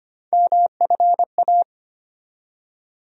Morse code, heard as a keyed tone, about 25 wpm, 700 Hz. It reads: MFA
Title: USA Call Sign Postfixes - 3 Letters 25wpm